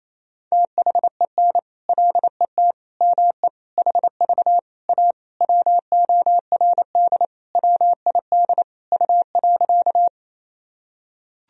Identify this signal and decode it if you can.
Morse code, heard as a keyed tone, about 28 wpm, 700 Hz. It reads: THEN LET ME HV A WORD WID U.
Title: Sentences from Top 300 Words and Abbreviations 28wpm